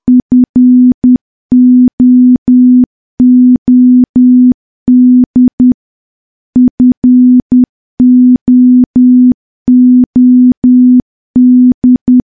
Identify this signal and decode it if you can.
Morse code, heard as a keyed tone, about 10 wpm, 250 Hz. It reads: FOOD FOOD
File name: morse / morse-training-10WPM